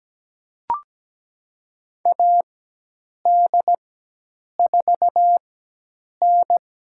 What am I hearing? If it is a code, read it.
Morse code, heard as a keyed tone, about 17 wpm, spaced as for 8 wpm, 700 Hz. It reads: AD4N
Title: USA Call Signs - 4x Character Spacing - 17wpm